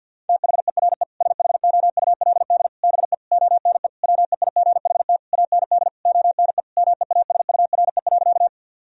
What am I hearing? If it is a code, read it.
Morse code, heard as a keyed tone, about 50 wpm, 700 Hz. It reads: THERE SHOULD BE ONE WEIGHT AND ONE MEASURE.
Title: Sentences from Top 1000 Words - Review Part 3 - 50wpm